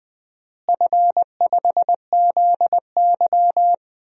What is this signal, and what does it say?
Morse code, heard as a keyed tone, about 21 wpm, 700 Hz. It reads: F5ZY
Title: DXCC-100 Call Signs - 4 Characters - Repeated 2x - 20wpm